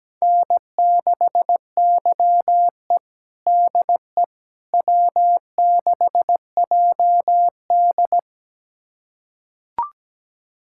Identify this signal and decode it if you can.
Morse code, heard as a keyed tone, about 17 wpm, 700 Hz. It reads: N6YE DE W6JD
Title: USA Callsign DE Callsign - 4 Characters - 17wpm